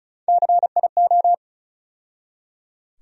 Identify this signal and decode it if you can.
Morse code, heard as a keyed tone, about 36 wpm, 700 Hz. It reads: CIO
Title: USA Call Sign Postfixes - 3 Letters 35wpm